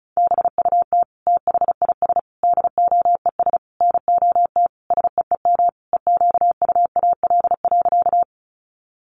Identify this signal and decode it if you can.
Morse code, heard as a keyed tone, about 34 wpm, 700 Hz. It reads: BUT THIS DOES NOT SEEM EQUAL.
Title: Sentences from Top 1400 Words 35wpm